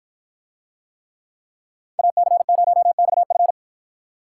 Morse code, heard as a keyed tone, about 53 wpm, 700 Hz. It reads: AC0XF